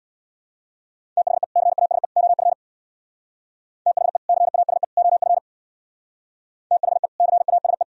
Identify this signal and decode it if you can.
Morse code, heard as a keyed tone, about 64 wpm, 700 Hz. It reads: THE CASE OF THE CASE OF THE CASE